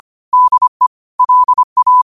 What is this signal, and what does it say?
Morse code, heard as a keyed tone, about 25 wpm, 1000 Hz. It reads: DE LA
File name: qotdfiles / 25wpm